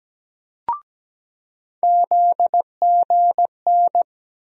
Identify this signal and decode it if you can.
Morse code, heard as a keyed tone, about 17 wpm, 700 Hz. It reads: ZGN
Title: USA Call Sign Postfixes - 3 Letters 17wpm